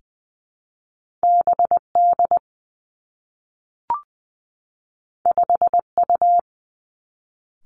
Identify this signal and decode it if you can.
Morse code, heard as a keyed tone, about 20 wpm, 700 Hz. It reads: B D 5 U
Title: Two Challenging Letter-Number Combinations 20wpm